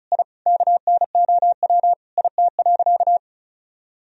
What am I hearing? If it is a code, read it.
Morse code, heard as a keyed tone, about 36 wpm, 700 Hz. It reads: I KNOW IT.
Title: Sentences from Top 1000 Words - Review Part 2 - 35wpm